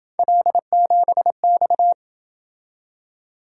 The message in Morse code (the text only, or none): L7X